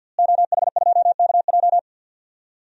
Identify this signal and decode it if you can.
Morse code, heard as a keyed tone, about 52 wpm, 700 Hz. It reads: OH2KJ
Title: DXCC-100 Call Signs - 5 Characters - Repeated 2x - 50wpm